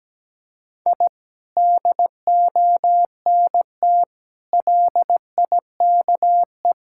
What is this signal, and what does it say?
Morse code, heard as a keyed tone, about 17 wpm, 700 Hz. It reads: I DONT LIKE
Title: Sentences from Top 1000 Words - Review Part 6 - 17wpm